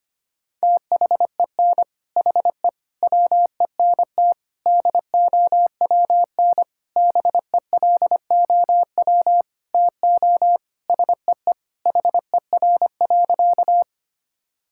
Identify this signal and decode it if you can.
Morse code, heard as a keyed tone, about 25 wpm, 700 Hz. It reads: THEN HE WENT DOWN BELOW TO SEE HER.
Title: Sentences from Top 1000 Words - Review Part 5 - 25wpm